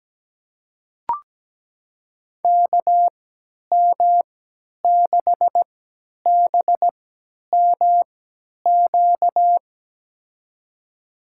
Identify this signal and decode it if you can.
Morse code, heard as a keyed tone, about 17 wpm, spaced as for 10 wpm, 700 Hz. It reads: KM6BMQ
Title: USA Call Signs - 6 Characters - 3x Character Spacing - 17wpm